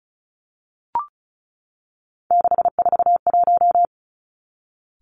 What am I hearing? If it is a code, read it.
Morse code, heard as a keyed tone, about 36 wpm, 700 Hz. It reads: 641
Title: Three Challenging Letter-Number Combinations 35wpm